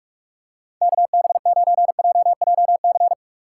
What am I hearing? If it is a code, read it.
Morse code, heard as a keyed tone, about 47 wpm, 700 Hz. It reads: KB9JJC
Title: USA Call Signs - 6 Characters - 4x Character Spacing - 45wpm